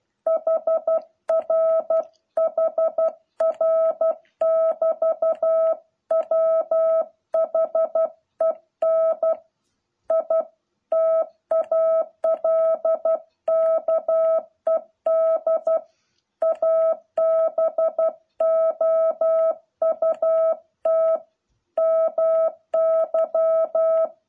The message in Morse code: HR HR <BT> WHEN I TALKED ABOUT MY

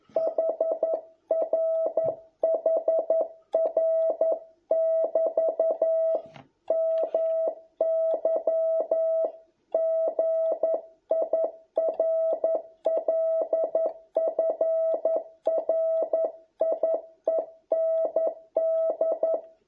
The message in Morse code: HR HR <BT> MY GIRLFRIEND

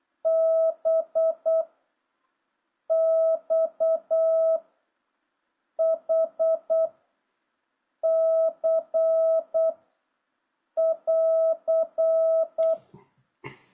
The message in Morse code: B X H C <AR>

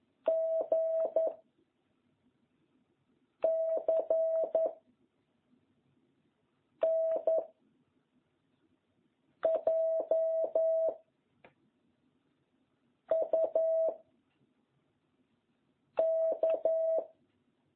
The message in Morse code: G C N J U K

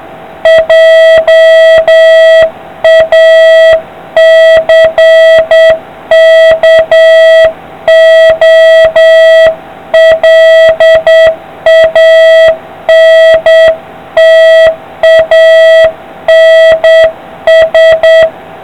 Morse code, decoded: JACKOLANTANS

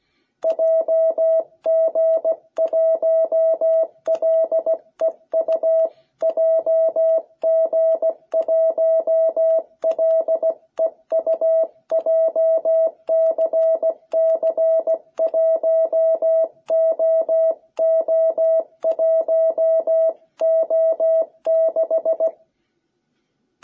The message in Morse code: JG1LEU JG1LEU JCC1OO1O6

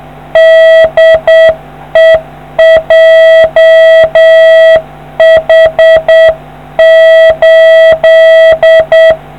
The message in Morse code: DE JH8

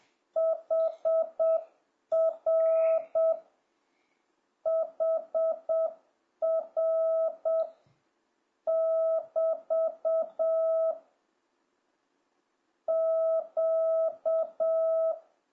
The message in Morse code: HR HR <BT> Q

